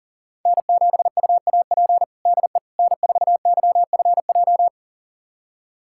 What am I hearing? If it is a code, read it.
Morse code, heard as a keyed tone, about 40 wpm, 700 Hz. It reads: N7UAP DE N4YFJ
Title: USA Callsign DE Callsign 40wpm